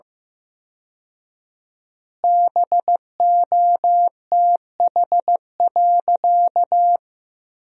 Morse code, heard as a keyed tone, about 15 wpm, 700 Hz. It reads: BOTH.
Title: 4x Word-Spacing - Sentences from Top 300 Words - 15wpm